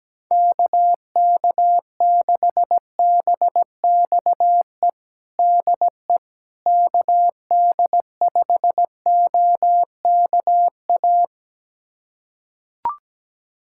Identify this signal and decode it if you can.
Morse code, heard as a keyed tone, about 17 wpm, 700 Hz. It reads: KK6BXE DE KD5OKA
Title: USA Callsign DE Callsign - 6 Characters - 17wpm